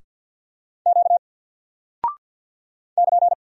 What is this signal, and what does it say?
Morse code, heard as a keyed tone, about 48 wpm, 700 Hz. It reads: Q <KN>
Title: Single Character (Letter:Number:Punctuation:Prosign) 50wpm